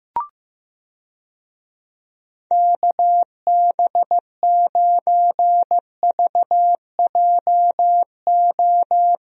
Morse code, heard as a keyed tone, about 15 wpm, 700 Hz. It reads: KB9VJO
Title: USA Call Signs - 6 Characters - Repeated 3x - 15wpm